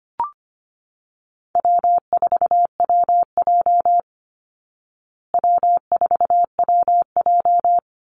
Morse code, heard as a keyed tone, about 25 wpm, 700 Hz. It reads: W4WJ W4WJ
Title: Easy Call Signs Repeated 2x - 25wpm